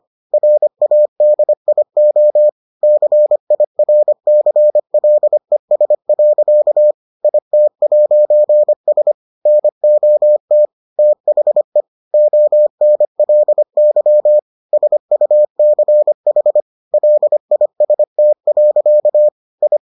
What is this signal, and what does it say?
Morse code, heard as a keyed tone, about 25 wpm, 600 Hz. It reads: RADIO CIRCLES. IT'S NOT THE ONLY SUCH LIST. I